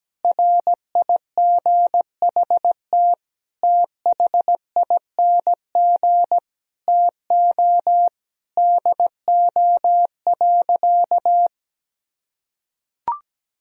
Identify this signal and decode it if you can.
Morse code, heard as a keyed tone, about 17 wpm, 700 Hz. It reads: RIGHT THING TO DO.
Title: Sentences from Top 1000 Words - Review Part 4 - 17wpm